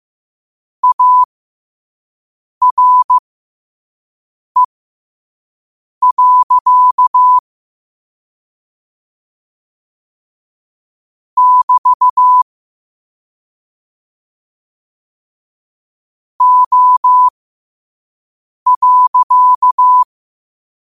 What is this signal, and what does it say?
Morse code, heard as a keyed tone, about 15 wpm, spaced as for 5 wpm, 1000 Hz. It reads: ARE. <BT> O.
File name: qotdfiles / 5wpm